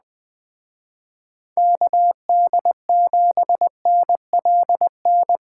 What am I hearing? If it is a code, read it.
Morse code, heard as a keyed tone, about 20 wpm, 700 Hz. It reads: KD7NLN
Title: Speed Racing - USA 2x3 Call Signs 17wpm